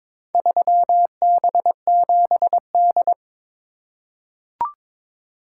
Morse code, heard as a keyed tone, about 22 wpm, 700 Hz. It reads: 3B7D
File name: practice / 22wpm / Four Letter-Number Combinations - 22wpm